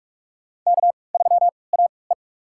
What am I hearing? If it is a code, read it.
Morse code, heard as a keyed tone, about 47 wpm, spaced as for 26 wpm, 700 Hz. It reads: K3AE
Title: USA Call Signs - 4 Characters - 3x Character Spacing - 45wpm